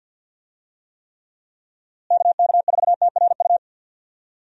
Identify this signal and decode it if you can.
Morse code, heard as a keyed tone, about 50 wpm, 700 Hz. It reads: KK4TRU
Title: USA Call Signs - 6 Characters - Repeated 3x - 50wpm